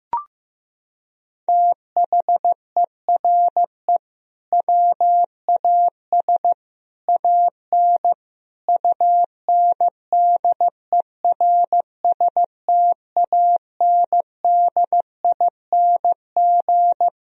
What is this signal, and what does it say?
Morse code, heard as a keyed tone, about 15 wpm, 700 Hz. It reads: THERE WAS AN UNDERSTANDING
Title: Sentences from Top 1000 Words - Review Part 3 - 15wpm